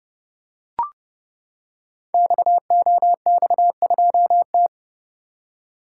Morse code, heard as a keyed tone, about 30 wpm, 700 Hz. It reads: XOX2T
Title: Code Groups - 5 Letter:Numbers - 30wpm